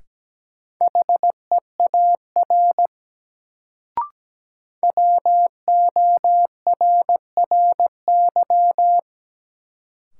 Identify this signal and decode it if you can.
Morse code, heard as a keyed tone, about 17 wpm, 700 Hz. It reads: HEAR WORRY
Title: Sets of 5 Words - New Word Drill - 17wpm